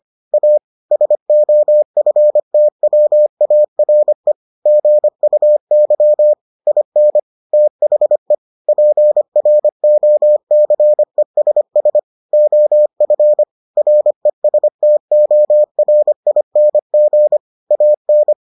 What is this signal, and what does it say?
Morse code, heard as a keyed tone, about 25 wpm, 600 Hz. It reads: A SOFTWARE GUY IN THE PROCESS OF RESTORING AN